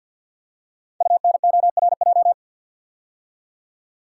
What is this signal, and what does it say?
Morse code, heard as a keyed tone, about 50 wpm, 690 Hz. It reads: UNORJ
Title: Code Groups - 5 Letters - 50wpm